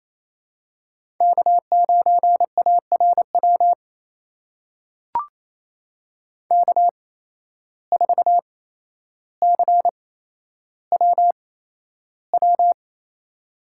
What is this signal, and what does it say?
Morse code, heard as a keyed tone, about 29 wpm, spaced as for 12 wpm, 700 Hz. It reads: K9ARW K4CWW
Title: USA Call Signs - 5 Characters - 8x Character Spacing - 28wpm